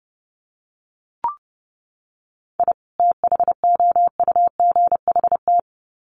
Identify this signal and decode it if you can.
Morse code, heard as a keyed tone, about 30 wpm, 700 Hz. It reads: I THOUGHT
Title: Sets of 2 Words 30wpm